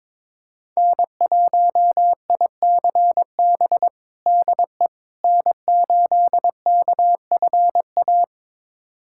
Morse code, heard as a keyed tone, about 22 wpm, 700 Hz. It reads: N1ICB DE N8KFA
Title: USA Callsign DE Callsign - 5 Characters - 22wpm